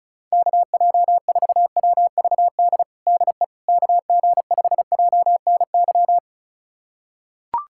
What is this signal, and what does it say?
Morse code, heard as a keyed tone, about 36 wpm, 700 Hz. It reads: KJ4WVD DE KG5JNY
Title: USA Callsign DE Callsign - 6 Characters - 35wpm